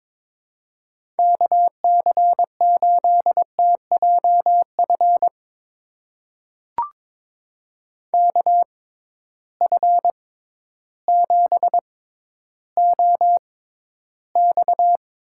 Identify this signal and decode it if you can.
Morse code, heard as a keyed tone, about 22 wpm, spaced as for 12 wpm, 700 Hz. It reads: KC8TJF KF7OX
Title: USA Call Signs - 6 Characters - 6x Character Spacing - 22wpm